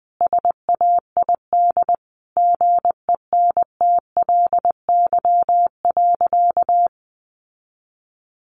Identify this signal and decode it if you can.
Morse code, heard as a keyed tone, about 20 wpm, 700 Hz. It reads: SAID GENTLY.
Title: Sentences from Top 1500 Words 20wpm